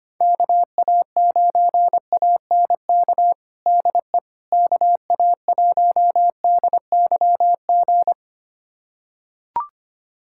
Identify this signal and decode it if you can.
Morse code, heard as a keyed tone, about 25 wpm, 700 Hz. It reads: KA9ANK DE KA1DYG
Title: USA Callsign DE Callsign - 6 Characters - 25wpm